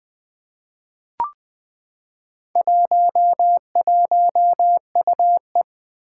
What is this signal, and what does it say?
Morse code, heard as a keyed tone, about 20 wpm, 700 Hz. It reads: 11UE
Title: Four Challenging Letter-Number Combinations 20wpm